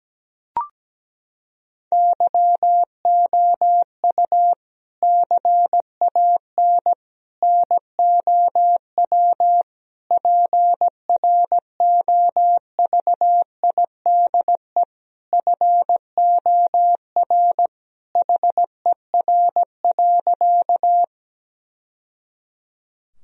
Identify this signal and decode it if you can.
Morse code, heard as a keyed tone, about 17 wpm, 700 Hz. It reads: YOU CAN NOW PROVIDE FOR HER.
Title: Sentences from Top 1000 Words - Review Part 3 - 17wpm